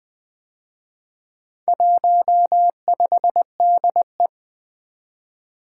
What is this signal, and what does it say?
Morse code, heard as a keyed tone, about 24 wpm, 700 Hz. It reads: 15DE
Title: Speed-Racing - Four Letter-Number Combinations - 20wpm